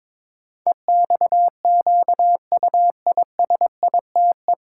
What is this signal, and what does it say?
Morse code, heard as a keyed tone, about 22 wpm, 700 Hz. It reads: EXQUISITE
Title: Pangram Challenge v2 - 22wpm